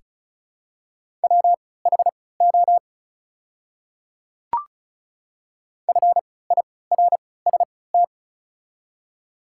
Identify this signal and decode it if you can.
Morse code, heard as a keyed tone, about 36 wpm, spaced as for 20 wpm, 700 Hz. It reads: WHO FIRST
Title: 3x Character Spacing - Top 100 Words - 35wpm